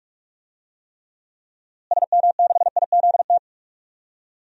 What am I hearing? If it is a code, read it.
Morse code, heard as a keyed tone, about 42 wpm, 700 Hz. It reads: SM6IZT
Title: DXCC-100 Call Signs - 6 Characters - Repeated 3x - 45wpm